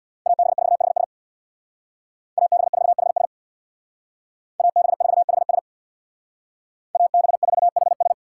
Morse code, heard as a keyed tone, about 61 wpm, 700 Hz. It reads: AB4HS AB4HS AB4HS AB4HS
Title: Speed Racing - USA 2x2 Call Signs 50wpm